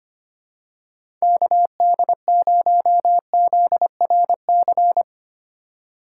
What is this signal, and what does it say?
Morse code, heard as a keyed tone, about 25 wpm, 700 Hz. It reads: KD0ZRC
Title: USA Call Signs - 6 Characters - 4x Character Spacing - 25wpm